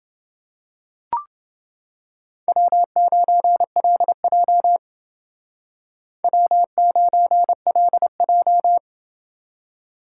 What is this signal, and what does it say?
Morse code, heard as a keyed tone, about 28 wpm, 700 Hz. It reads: W9LJ W9LJ